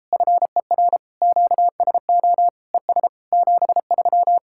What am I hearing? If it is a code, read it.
Morse code, heard as a keyed tone, about 37 wpm, 700 Hz. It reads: FER QSO ES 73